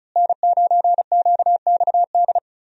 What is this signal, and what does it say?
Morse code, heard as a keyed tone, about 34 wpm, 700 Hz. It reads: N9QXD
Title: USA Call Signs - 5 Characters - Repeated 2x - 35wpm